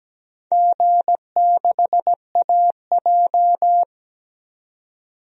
Morse code, heard as a keyed tone, about 17 wpm, 700 Hz. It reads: G6AJ
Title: DXCC-100 Call Signs - 4 Characters - 17wpm